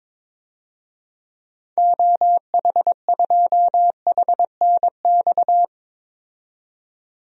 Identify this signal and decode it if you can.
Morse code, heard as a keyed tone, about 22 wpm, 700 Hz. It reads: OH2HNX
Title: DXCC-100 Call Signs - 6 Characters - 22wpm